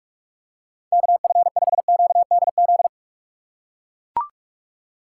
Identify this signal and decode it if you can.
Morse code, heard as a keyed tone, about 41 wpm, 700 Hz. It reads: KF5QDZ
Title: USA Call Signs - 6 Characters - 2x Character Spacing - 45wpm